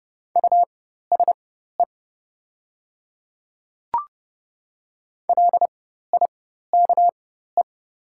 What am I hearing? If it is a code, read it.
Morse code, heard as a keyed tone, about 30 wpm, spaced as for 14 wpm, 700 Hz. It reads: USE LIKE
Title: 4x Character Spacing - Top 100 Words - 30wpm